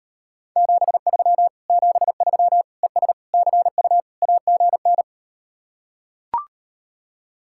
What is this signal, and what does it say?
Morse code, heard as a keyed tone, about 38 wpm, 700 Hz. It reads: 73 73 ES CU AGN